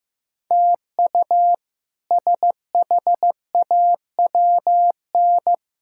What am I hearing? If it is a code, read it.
Morse code, heard as a keyed tone, about 16 wpm, 700 Hz. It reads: TU SHAWN
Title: K1USN SST QSO Practice - 15wpm